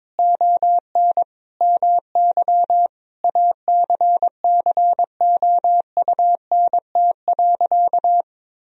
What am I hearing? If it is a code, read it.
Morse code, heard as a keyed tone, about 22 wpm, 700 Hz. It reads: ON MY ACCOUNT.
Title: Sentences from Top 1000 Words - Review Part 3 - 22wpm